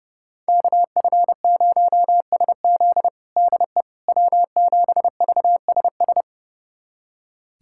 Morse code, heard as a keyed tone, about 30 wpm, 700 Hz. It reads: KF0SZ DE W7VSS
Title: USA Callsign DE Callsign - 5 Characters - 30wpm